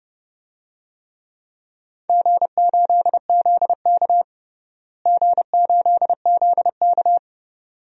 Morse code, heard as a keyed tone, about 30 wpm, 700 Hz. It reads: G8ZK G8ZK
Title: DXCC-100 Call Signs - 4 Characters - Repeated 2x - 30wpm